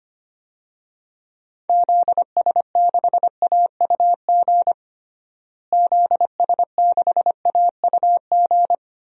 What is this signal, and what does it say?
Morse code, heard as a keyed tone, about 25 wpm, 700 Hz. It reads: ZS6AUG ZS6AUG
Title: DXCC-100 Call Signs - 6 Characters - Repeated 2x - 25wpm